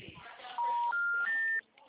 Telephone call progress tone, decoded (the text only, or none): special information tone